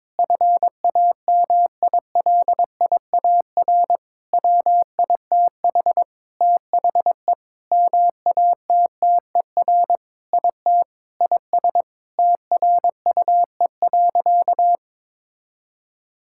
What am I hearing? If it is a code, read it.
Morse code, heard as a keyed tone, about 22 wpm, 700 Hz. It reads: FAMILIAR WITH THE MATTER IT IS TRUE.
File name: Sentences with 7-Word Phrases - 22wpm